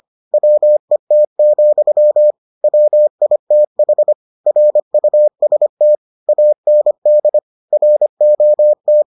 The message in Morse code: WET, WITH RUST AND ROT

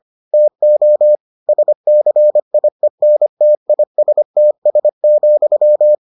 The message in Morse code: TO SCIENTISTS,

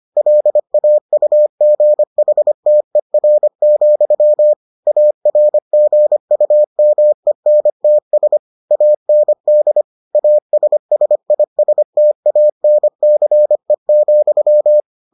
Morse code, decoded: LAUGHTER, ARGUMENTS AND ASSISTANCE,

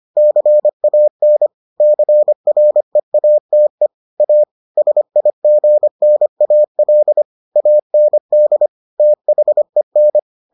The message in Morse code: CAN CREATE A SIGNAL AND THEN

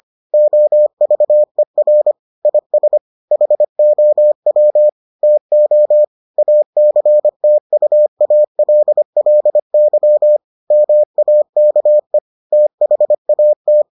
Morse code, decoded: OVER IS HOW TO ACTUALLY MAKE THAT

